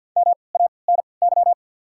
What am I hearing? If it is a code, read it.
Morse code, heard as a keyed tone, about 48 wpm, spaced as for 28 wpm, 700 Hz. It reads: MANY
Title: 3x Character Spacing - Top 100 Words - 50wpm